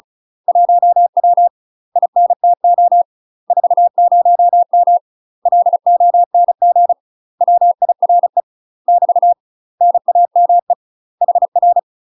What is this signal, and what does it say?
Morse code, heard as a keyed tone, about 36 wpm, 700 Hz. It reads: 1W INTO 40M LONG WIRE <BT> NAME HR
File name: UK MASTER QSOs TRAINING / 35WPM / UK MASTER QSOs 0026